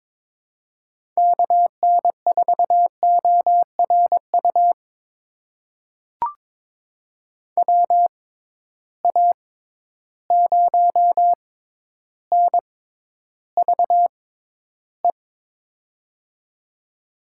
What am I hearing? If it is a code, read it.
Morse code, heard as a keyed tone, about 21 wpm, spaced as for 11 wpm, 700 Hz. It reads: KN4ORU WA0NVE
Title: USA Call Signs - 6 Characters - 6x Character Spacing - 22wpm